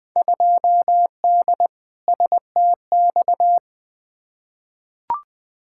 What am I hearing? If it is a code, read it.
Morse code, heard as a keyed tone, about 20 wpm, 700 Hz. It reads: 2D STX
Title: Field Day Exchanges - 20wpm